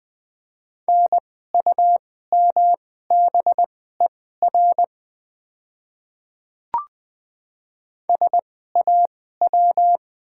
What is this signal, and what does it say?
Morse code, heard as a keyed tone, about 20 wpm, spaced as for 15 wpm, 700 Hz. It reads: NUMBER SAW
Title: 2x Character Spacing - Top 300 Words - 20wpm